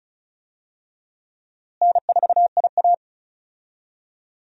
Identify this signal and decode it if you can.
Morse code, heard as a keyed tone, about 36 wpm, 700 Hz. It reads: N4IA
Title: USA Call Signs - 4 Characters - Repeated 2x - 35wpm